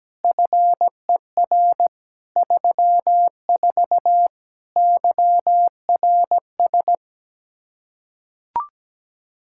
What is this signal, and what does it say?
Morse code, heard as a keyed tone, about 17 wpm, 700 Hz. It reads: FER 34 YRS
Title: Practice QSO - Exchange 3 - More Info - 17wpm